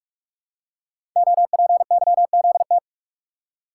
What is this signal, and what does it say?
Morse code, heard as a keyed tone, about 44 wpm, 700 Hz. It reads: OPYZT
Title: Code Groups - 5 Letters - 45wpm